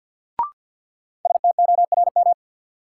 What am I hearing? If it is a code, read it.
Morse code, heard as a keyed tone, about 48 wpm, 700 Hz. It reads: STORM
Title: Sentences from Top 1500 Words - New Word Drill - 50wpm